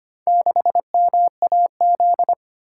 Morse code, heard as a keyed tone, about 25 wpm, 700 Hz. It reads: 6MAZ
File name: practice / 25wpm / Four Letter-Number Combinations - 25wpm